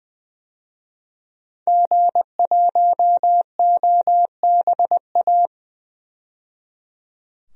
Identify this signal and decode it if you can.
Morse code, heard as a keyed tone, about 20 wpm, 700 Hz. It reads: G1OBA
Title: DXCC-100 Call Signs - 5 Characters - 20wpm